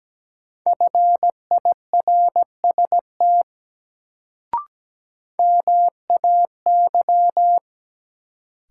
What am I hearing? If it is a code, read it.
Morse code, heard as a keyed tone, about 17 wpm, 700 Hz. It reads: FIRST MAY
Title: Sentences from Top 100 Words - New Word Drill - 17wpm